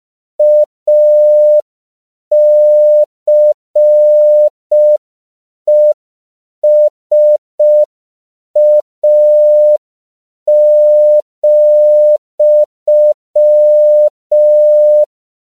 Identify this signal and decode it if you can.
Morse code, heard as a keyed tone, about 5 wpm, 600 Hz. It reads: ACESA,
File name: CW / 5PPM